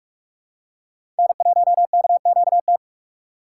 Morse code, heard as a keyed tone, about 41 wpm, 700 Hz. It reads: N1KQT
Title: USA Call Signs - 5 Characters - 6x Character Spacing - 45wpm